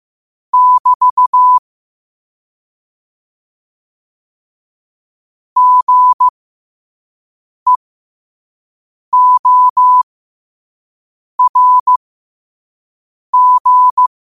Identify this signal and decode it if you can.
Morse code, heard as a keyed tone, about 15 wpm, spaced as for 5 wpm, 1000 Hz. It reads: <BT> GEORG
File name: qotdfiles / 5wpm